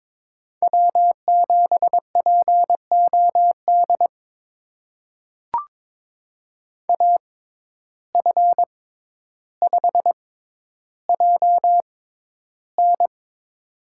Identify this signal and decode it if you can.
Morse code, heard as a keyed tone, about 22 wpm, spaced as for 11 wpm, 700 Hz. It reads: W7POD AF5JN
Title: USA Call Signs - 5 Characters - 6x Character Spacing - 22wpm